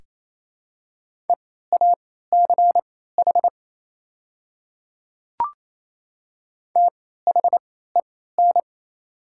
Morse code, heard as a keyed tone, about 28 wpm, spaced as for 16 wpm, 700 Hz. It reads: EACH THEN
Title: 3x Character Spacing - Top 100 Words - 28wpm